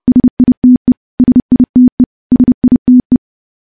Morse code, heard as a keyed tone, about 30 wpm, 250 Hz. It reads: SITE SITE SITE
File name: morse / morse-training-30WPM